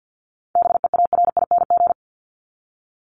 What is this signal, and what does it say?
Morse code, heard as a keyed tone, about 49 wpm, 700 Hz. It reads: BEARING
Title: ING Suffix - Top 100 Words with Base Words 50wpm